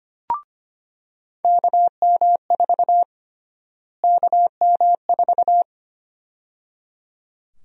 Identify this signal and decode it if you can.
Morse code, heard as a keyed tone, about 25 wpm, 700 Hz. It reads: KM4 KM4
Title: USA Call Sign Prefixes - Repeated 2x - 25wpm